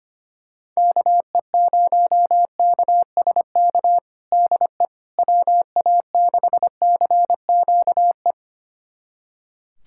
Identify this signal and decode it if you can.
Morse code, heard as a keyed tone, about 25 wpm, 700 Hz. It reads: KE0KSK DE WA6CQE
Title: USA Callsign DE Callsign 25wpm